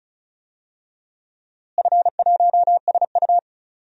Morse code, heard as a keyed tone, about 36 wpm, 700 Hz. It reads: F1SU
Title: DXCC-100 Call Signs - 4 Characters - Repeated 2x - 35wpm